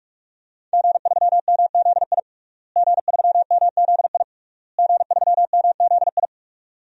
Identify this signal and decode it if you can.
Morse code, heard as a keyed tone, about 42 wpm, 700 Hz. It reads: G3MZI G3MZI G3MZI
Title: DXCC-100 Call Signs - 5 Characters - Repeated 3x - 45wpm